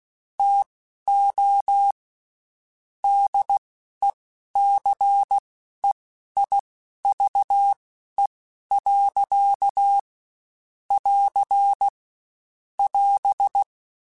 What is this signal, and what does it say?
Morse code, heard as a keyed tone, about 16 wpm, spaced as for 11 wpm, 800 Hz. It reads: TO DECEIVE. <AR> <AS>
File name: Quotes Spaced / Quotes Spaced-mp3s / Deceive sp @15